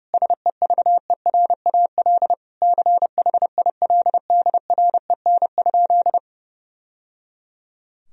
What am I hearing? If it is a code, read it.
Morse code, heard as a keyed tone, about 32 wpm, 700 Hz. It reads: SEVERAL CHILDREN?
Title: Sentences from Top 1000 Words - Review Part 2 - 30wpm